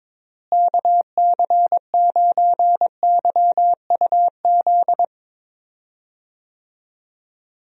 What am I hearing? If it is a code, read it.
Morse code, heard as a keyed tone, about 23 wpm, 700 Hz. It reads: KC9YUZ
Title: USA Call Signs - 6 Characters - Repeated 2x - 22wpm